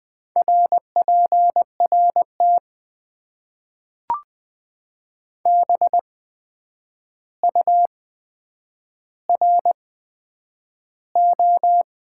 Morse code, heard as a keyed tone, about 20 wpm, spaced as for 9 wpm, 700 Hz. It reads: RPRT BURO
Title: Top 90 QSO Elements - 8x Character Spacing - 20wpm